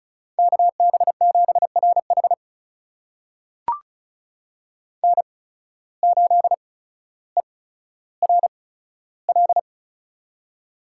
Morse code, heard as a keyed tone, about 36 wpm, spaced as for 15 wpm, 700 Hz. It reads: KB7RH N8ERL
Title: USA Call Signs - 8x Character Spacing - 35wpm